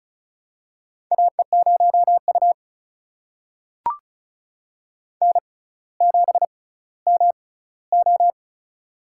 Morse code, heard as a keyed tone, about 36 wpm, spaced as for 18 wpm, 700 Hz. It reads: AE0U N7MO
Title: USA Call Signs - 4 Characters - 6x Character Spacing - 35wpm